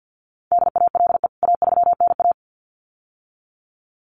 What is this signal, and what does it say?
Morse code, heard as a keyed tone, about 50 wpm, 700 Hz. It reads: DALE A3NA
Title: CWT Exchange 50wpm